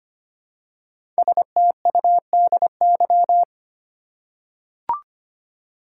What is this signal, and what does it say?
Morse code, heard as a keyed tone, about 25 wpm, 700 Hz. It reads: STUDY
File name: 5 Letter Words - First 3 Letters Given - 25wpm